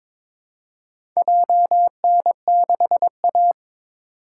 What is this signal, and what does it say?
Morse code, heard as a keyed tone, about 21 wpm, 700 Hz. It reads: JN6A
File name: practice / 22wpm / Four Letter-Number Combinations - 22wpm